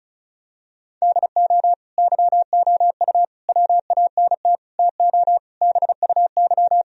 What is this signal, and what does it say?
Morse code, heard as a keyed tone, about 34 wpm, 700 Hz. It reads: DO YOU WANT TO BUY
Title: Sentences from Top 1000 Words - Review Part 4 - 35wpm